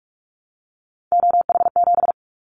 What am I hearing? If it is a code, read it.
Morse code, heard as a keyed tone, about 44 wpm, 700 Hz. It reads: OH7
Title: Callsign Prefixes for Top 25 DXCC - Repeated 3x - 45wpm